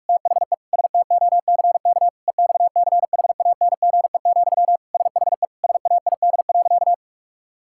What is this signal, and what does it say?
Morse code, heard as a keyed tone, about 46 wpm, 700 Hz. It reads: THE STOCK EXCHANGE, SHE SAID.
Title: Sentences from Top 1000 Words - Review Part 6 - 45wpm